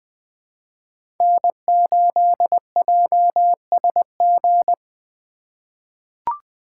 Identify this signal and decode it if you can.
Morse code, heard as a keyed tone, about 21 wpm, 700 Hz. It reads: N8JSG
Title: USA Call Signs - 5 Characters - 2x Character Spacing - 20wpm